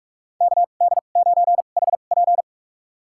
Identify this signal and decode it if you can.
Morse code, heard as a keyed tone, about 45 wpm, spaced as for 32 wpm, 700 Hz. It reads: KD9HP
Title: USA Call Signs - 5 Characters - 2x Character Spacing - 45wpm